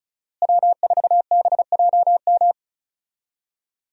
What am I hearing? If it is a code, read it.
Morse code, heard as a keyed tone, about 36 wpm, 700 Hz. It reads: W4BJM
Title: USA Call Signs - 5 Characters - Repeated 3x - 35wpm